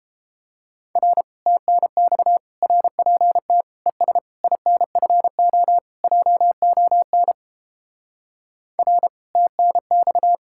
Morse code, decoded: R TNX RPT ES INFO JON R TNX

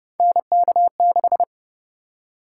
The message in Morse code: NK6

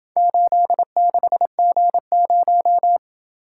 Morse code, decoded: 86G0

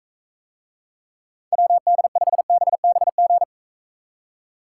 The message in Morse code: WD5BBG

